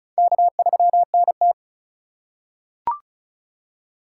Morse code, heard as a keyed tone, about 36 wpm, 700 Hz. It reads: K3NT
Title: USA Call Signs - 4 Characters - 3x Character Spacing - 35wpm